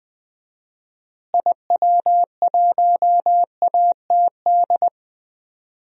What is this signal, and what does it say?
Morse code, heard as a keyed tone, about 20 wpm, 700 Hz. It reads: IW1ATD
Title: DXCC-100 Call Signs - 6 Characters - Repeated 3x - 20wpm